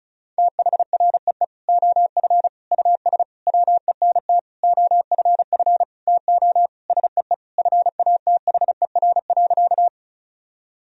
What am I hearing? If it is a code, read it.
Morse code, heard as a keyed tone, about 36 wpm, 700 Hz. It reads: THREE OF US WENT OFF TO SEE FATHER.
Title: Sentences from Top 1000 Words - Review Part 1 - 35wpm